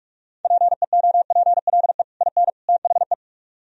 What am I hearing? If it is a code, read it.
Morse code, heard as a keyed tone, about 44 wpm, 700 Hz. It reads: PEOPLE IN THE